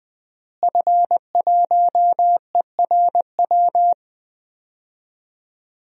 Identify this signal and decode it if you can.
Morse code, heard as a keyed tone, about 20 wpm, 700 Hz. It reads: F1ERW
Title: DXCC-100 Call Signs - 5 Characters - Repeated 2x - 20wpm